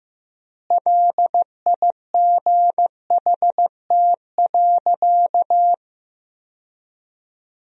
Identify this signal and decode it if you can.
Morse code, heard as a keyed tone, about 15 wpm, 700 Hz. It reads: LIGHT.
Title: 3x Word-Spacing - Sentences from Top 200 Words - 15wpm